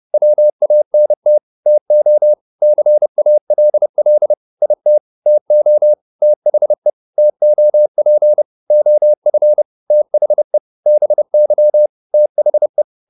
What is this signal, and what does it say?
Morse code, heard as a keyed tone, about 30 wpm, 600 Hz. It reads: WANT TO CALL IT TO THE TOP OF THE BY THE